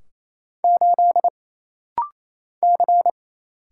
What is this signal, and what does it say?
Morse code, heard as a keyed tone, about 27 wpm, 700 Hz. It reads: 8 C
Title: Single Letter-Number - Rapid-Fire 28wpm